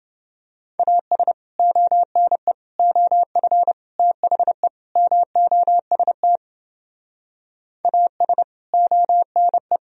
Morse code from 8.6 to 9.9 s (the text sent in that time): ONE